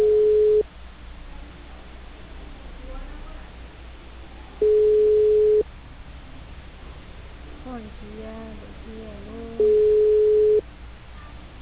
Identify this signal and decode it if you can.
Ringback tone (Europe, 425 Hz), with speech.